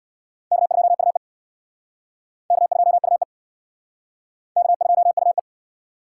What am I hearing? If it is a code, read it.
Morse code, heard as a keyed tone, about 65 wpm, 700 Hz. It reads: K2UE K2UE K2UE